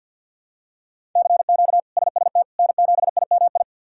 Morse code, heard as a keyed tone, about 53 wpm, 700 Hz. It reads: CQ SST N7IMI
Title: K1USN SST QSO Practice - 50wpm